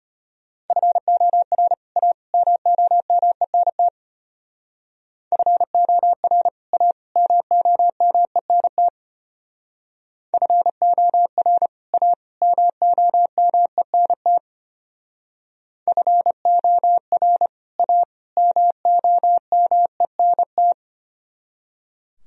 Morse code, heard as a keyed tone, about 31 wpm, 700 Hz. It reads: FOR A MOMENT FOR A MOMENT FOR A MOMENT FOR A MOMENT
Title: Speed-Racing - Sets of 3 Words 25wpm